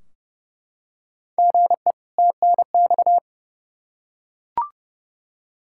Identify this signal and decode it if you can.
Morse code, heard as a keyed tone, about 30 wpm, 700 Hz. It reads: GE TNX
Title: Practice QSO - Exchange 1 - Beginning - 30wpm